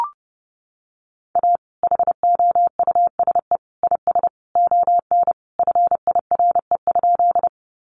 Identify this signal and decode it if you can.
Morse code, heard as a keyed tone, about 29 wpm, 700 Hz. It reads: A HOUSE IS ON FIRE?
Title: Sentences from Top 400 Words 30wpm